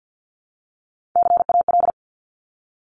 Morse code, heard as a keyed tone, about 50 wpm, 700 Hz. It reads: CAL
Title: Top Three Letter Combinations in Words 50wpm